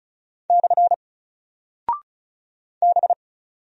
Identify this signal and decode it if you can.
Morse code, heard as a keyed tone, about 36 wpm, 700 Hz. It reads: / B
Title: Single Character (Letter:Number:Punctuation:Prosign) 35wpm